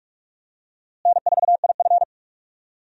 Morse code, heard as a keyed tone, about 41 wpm, 700 Hz. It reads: N4IF
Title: USA Call Signs - 4 Characters - Repeated 2x - 45wpm